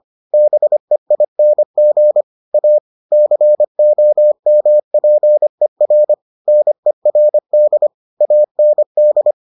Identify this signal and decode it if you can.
Morse code, heard as a keyed tone, about 25 wpm, 600 Hz. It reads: BEING A COMPER NERD AND